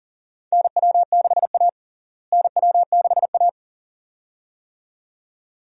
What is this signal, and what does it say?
Morse code, heard as a keyed tone, about 37 wpm, 700 Hz. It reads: NW6A NW6A
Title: USA Call Signs - 4 Characters - Repeated 2x - 40wpm